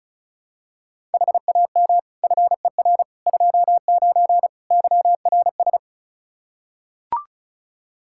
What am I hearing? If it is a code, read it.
Morse code, heard as a keyed tone, about 34 wpm, 700 Hz. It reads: HAM FER 29 YRS
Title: Practice QSO - Exchange 3 - More Info - 35wpm